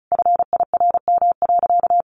Morse code, heard as a keyed tone, about 34 wpm, 700 Hz. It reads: FIRM.
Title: Sentences from Top 1400 Words 35wpm